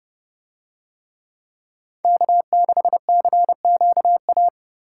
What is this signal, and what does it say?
Morse code, heard as a keyed tone, about 29 wpm, 700 Hz. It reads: K6CQA